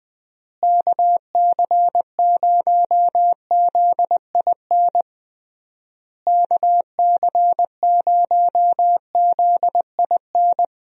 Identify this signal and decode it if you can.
Morse code, heard as a keyed tone, about 21 wpm, 700 Hz. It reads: KC0ZIN KC0ZIN
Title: USA Call Signs - 6 Characters - Repeated 3x - 20wpm